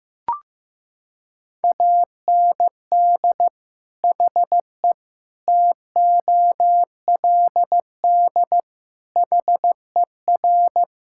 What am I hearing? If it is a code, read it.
Morse code, heard as a keyed tone, about 15 wpm, 700 Hz. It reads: AND HE TOLD HER
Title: Sentences from Top 1000 Words - Review Part 5 - 15wpm